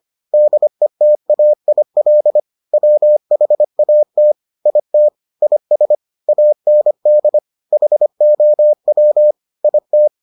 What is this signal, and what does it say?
Morse code, heard as a keyed tone, about 25 wpm, 600 Hz. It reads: DETAIL WHAT IT IS AND HOW IT